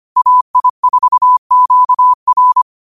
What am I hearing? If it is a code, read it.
Morse code, heard as a keyed tone, about 25 wpm, 1000 Hz. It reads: AI4QR
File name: qotdfiles / 25wpm